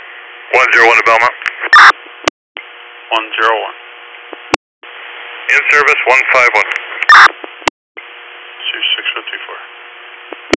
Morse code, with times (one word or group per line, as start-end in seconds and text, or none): none